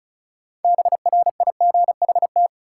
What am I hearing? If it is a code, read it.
Morse code, heard as a keyed tone, about 36 wpm, 700 Hz. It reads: BRIGHT
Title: Sentences from Top 1000 Words - New Word Drill - 35wpm